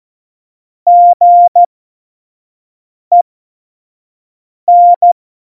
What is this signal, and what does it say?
Morse code, heard as a keyed tone, about 14 wpm, spaced as for 5 wpm, 700 Hz. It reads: GEN
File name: Morsekurs / Witze / Witz 14 Juli 2020, 25 F70